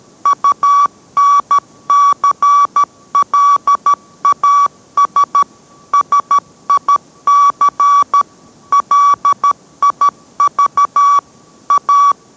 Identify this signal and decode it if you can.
Morse code, heard as a keyed tone, about 13 wpm, 1140 Hz. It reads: UNCLAS SIC LIV A